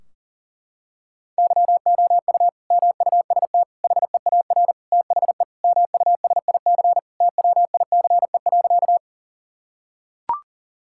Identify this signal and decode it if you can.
Morse code, heard as a keyed tone, about 40 wpm, 700 Hz. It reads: YOU MUST HEAR THE MUSIC TWICE.
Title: Sentences from Top 1600 Words 40wpm